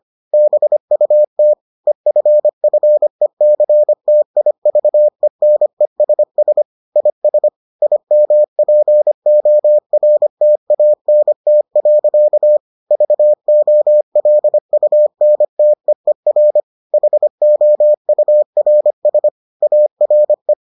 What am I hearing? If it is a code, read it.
Morse code, heard as a keyed tone, about 25 wpm, 600 Hz. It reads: BUT EFFECTIVENESS IS IMPORTANT. VOLUNTEER HOURS ARE